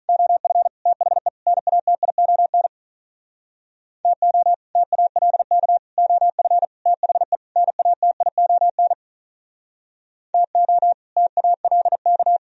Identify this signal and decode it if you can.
Morse code, heard as a keyed tone, about 41 wpm, 700 Hz. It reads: OF THE NATION TO TALK OF THE NATION TO TALK
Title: Speed-Racing - Sets of 5 Words 35wpm